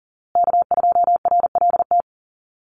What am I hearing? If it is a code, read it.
Morse code, heard as a keyed tone, about 40 wpm, 700 Hz. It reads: K2RLT
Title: USA Call Signs Repeated 2x 40wpm